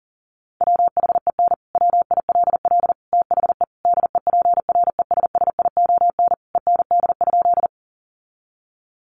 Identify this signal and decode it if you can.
Morse code, heard as a keyed tone, about 37 wpm, 700 Hz. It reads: WHEN WILL THE DEPRESSION END?
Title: Sentences from Top 1300 Words 40wpm